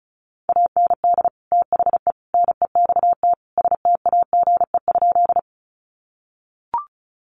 Morse code, heard as a keyed tone, about 34 wpm, 700 Hz. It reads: AND THE NEXT STAGE?
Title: Sentences from Top 1200 Words 35wpm